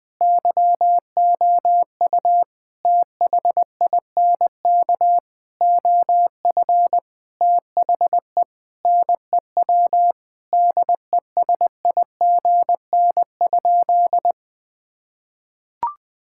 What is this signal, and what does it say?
Morse code, heard as a keyed tone, about 20 wpm, 700 Hz. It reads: YOU THINK OF THE NEW DESIGN?
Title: Sentences with 5-Word Phrases - Spoken Beginning - 20wpm